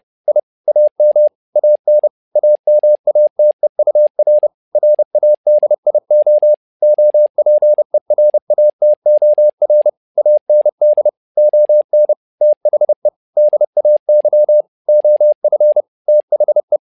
Morse code, heard as a keyed tone, about 30 wpm, 600 Hz. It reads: I AM AN AMATEUR RADIO OPERATOR AND ON THE DAY OF THE
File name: learning center / tower trouble / tower trouble 30wpm